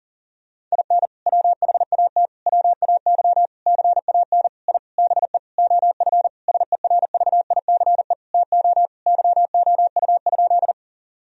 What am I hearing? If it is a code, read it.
Morse code, heard as a keyed tone, about 40 wpm, 700 Hz. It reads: IN WHAT WAY CAN I BE OF SERVICE TO YOU?
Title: Sentences from Top 1000 Words - Review Part 4 - 40wpm